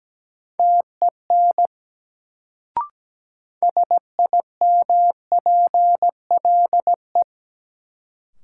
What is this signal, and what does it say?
Morse code, heard as a keyed tone, about 17 wpm, 700 Hz. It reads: TEN SIMPLE
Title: Sentences from Top 400 Words - New Word Drill - 17wpm